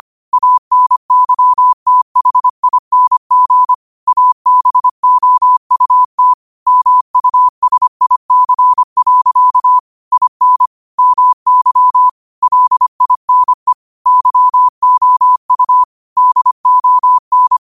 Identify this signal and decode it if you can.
Morse code, heard as a keyed tone, about 25 wpm, 1000 Hz. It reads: ANYTHING ABOUT MUSIC. IN MY LINE YOU DON
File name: qotdfiles / 25wpm